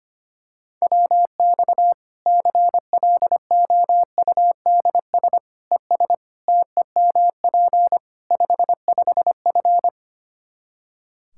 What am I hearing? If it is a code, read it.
Morse code, heard as a keyed tone, about 25 wpm, 700 Hz. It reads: WX CLOUDS ES TEMP 55F
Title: K1USN SST QSO Practice 25wpm